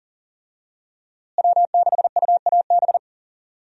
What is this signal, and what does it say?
Morse code, heard as a keyed tone, about 37 wpm, 700 Hz. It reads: W6UAB
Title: USA Call Signs - 5 Characters - Repeated 2x - 40wpm